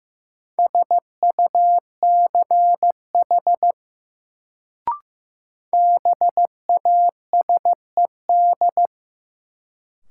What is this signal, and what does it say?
Morse code, heard as a keyed tone, about 16 wpm, 700 Hz. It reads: SUCH BASED
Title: Sets of 2 Words - New Word Drill - 15wpm